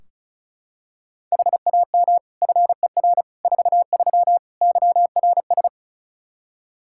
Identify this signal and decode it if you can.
Morse code, heard as a keyed tone, about 34 wpm, 700 Hz. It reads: HAM FER 43 YRS
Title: Practice QSO - Exchange 3 - More Info - 35wpm